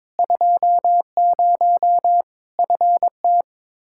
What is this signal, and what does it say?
Morse code, heard as a keyed tone, about 22 wpm, 700 Hz. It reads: 20 FT
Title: Practice QSO - Exchange 2 - Info - 22wpm